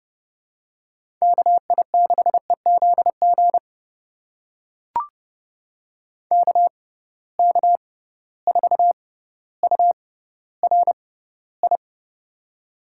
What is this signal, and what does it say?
Morse code, heard as a keyed tone, about 30 wpm, spaced as for 15 wpm, 700 Hz. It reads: KI6EZG KK4URI
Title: USA Call Signs - 6 Characters - 6x Character Spacing - 30wpm